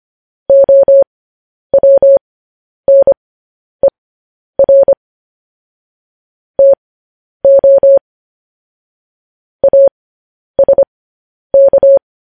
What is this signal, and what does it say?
Morse code, heard as a keyed tone, about 25 wpm, spaced as for 10 wpm, 550 Hz. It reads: OWNER TO ASK
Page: News Headlines in Morse Code at 10 WPM